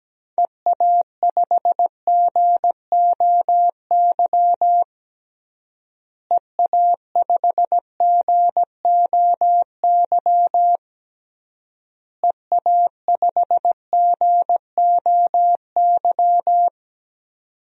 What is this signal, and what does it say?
Morse code, heard as a keyed tone, about 17 wpm, 700 Hz. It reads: EA5GOY EA5GOY EA5GOY
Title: DXCC-100 Call Signs - 6 Characters - Repeated 3x - 17wpm